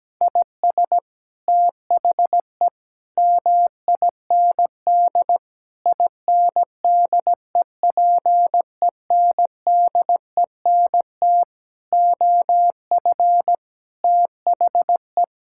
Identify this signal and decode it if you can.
Morse code, heard as a keyed tone, about 17 wpm, 700 Hz. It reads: IS THE MIND INDEPENDENT OF THE
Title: Sentences from Top 1000 Words - Review Part 6 - 17wpm